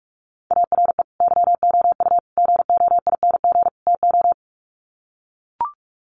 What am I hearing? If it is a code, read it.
Morse code, heard as a keyed tone, about 44 wpm, 700 Hz. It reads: ARE YOU GOING TO
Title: Sets of 4 Words 45wpm